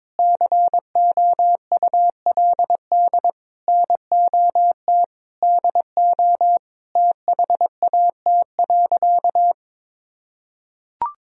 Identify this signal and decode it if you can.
Morse code, heard as a keyed tone, about 22 wpm, 700 Hz. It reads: COULD NOT DO THAT.
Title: Sentences from Top 1000 Words - Review Part 4 - 22wpm